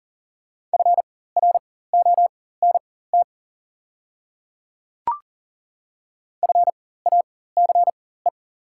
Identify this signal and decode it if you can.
Morse code, heard as a keyed tone, about 37 wpm, spaced as for 18 wpm, 700 Hz. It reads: FRONT FACE
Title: 4x Character Spacing - Top 500 Words - 40wpm